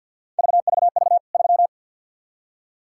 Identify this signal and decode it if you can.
Morse code, heard as a keyed tone, about 53 wpm, 700 Hz. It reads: VVV 3
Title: K1USN SST QSO Practice 50wpm